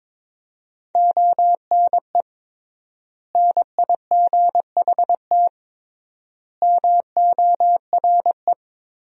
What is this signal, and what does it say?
Morse code, heard as a keyed tone, about 22 wpm, 700 Hz. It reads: ONE NIGHT MORE
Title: 3x Word-Spacing - Sentences from Top 300 Words - 22wpm